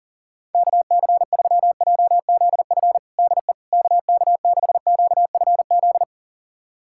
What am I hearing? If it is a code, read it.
Morse code, heard as a keyed tone, about 40 wpm, 700 Hz. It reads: KC3JZF DE KK6QFZ
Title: USA Callsign DE Callsign 40wpm